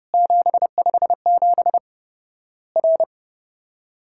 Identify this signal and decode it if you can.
Morse code, heard as a keyed tone, about 30 wpm, 700 Hz. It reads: 757 R
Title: Worked All Europe DX Contest - QTC Exchange - 30wpm